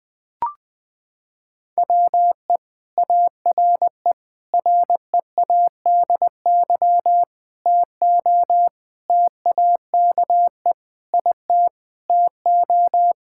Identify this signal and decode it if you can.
Morse code, heard as a keyed tone, about 21 wpm, 700 Hz. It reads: WE ARE READY TO TAKE IT TO
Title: Sentences with 7-Word Phrases - 20wpm